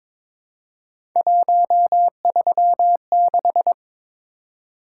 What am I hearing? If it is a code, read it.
Morse code, heard as a keyed tone, about 22 wpm, 700 Hz. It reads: 136
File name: practice / 22wpm / Three Letter-Number Combinations - 22wpm